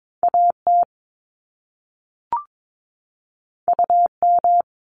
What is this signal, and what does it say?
Morse code, heard as a keyed tone, about 22 wpm, 700 Hz. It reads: AT UM
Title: Top Two Letter Combinations in Words 22wpm